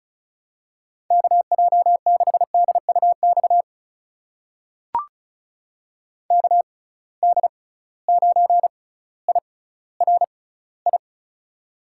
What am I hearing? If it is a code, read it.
Morse code, heard as a keyed tone, about 36 wpm, spaced as for 18 wpm, 700 Hz. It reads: KJ6DUX KD9IRI
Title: USA Call Signs - 6 Characters - 6x Character Spacing - 35wpm